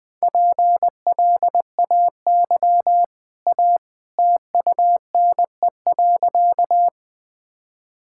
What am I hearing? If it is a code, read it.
Morse code, heard as a keyed tone, about 21 wpm, 700 Hz. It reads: PLAY A TUNE.
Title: Easy Sentences with Hints - 20wpm